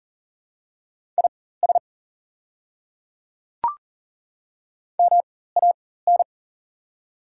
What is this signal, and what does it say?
Morse code, heard as a keyed tone, about 40 wpm, spaced as for 19 wpm, 700 Hz. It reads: IS MAN
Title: 4x Character Spacing - Top 300 Words - 40wpm